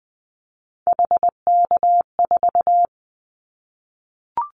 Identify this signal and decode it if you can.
Morse code, heard as a keyed tone, about 20 wpm, 700 Hz. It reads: HK4
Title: Callsign Prefixes for Top 100 DXCC - 20wpm